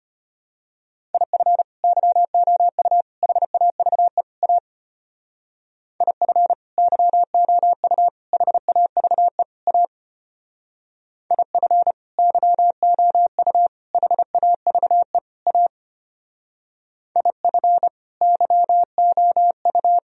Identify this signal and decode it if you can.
Morse code, heard as a keyed tone, about 31 wpm, 700 Hz. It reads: IF YOU HAVE A IF YOU HAVE A IF YOU HAVE A IF YOU